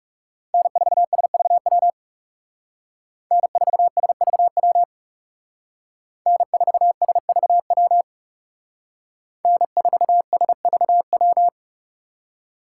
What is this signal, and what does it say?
Morse code, heard as a keyed tone, about 38 wpm, 700 Hz. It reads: N4SVW N4SVW N4SVW N4SVW